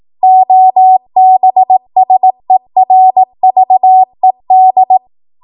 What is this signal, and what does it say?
Morse code, heard as a keyed tone, about 18 wpm, 750 Hz. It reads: OBSERVED
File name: Morse / 18WPM